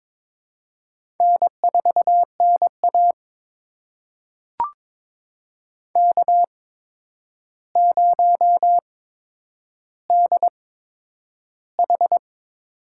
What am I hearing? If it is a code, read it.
Morse code, heard as a keyed tone, about 22 wpm, spaced as for 9 wpm, 700 Hz. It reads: N4NA K0DH
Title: USA Call Signs - 4 Characters - 8x Character Spacing - 22wpm